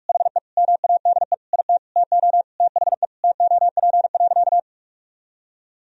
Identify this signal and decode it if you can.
Morse code, heard as a keyed tone, about 46 wpm, 700 Hz. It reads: HE MADE IT TO THE TOP.
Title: Sentences from Top 1000 Words - Review Part 2 - 45wpm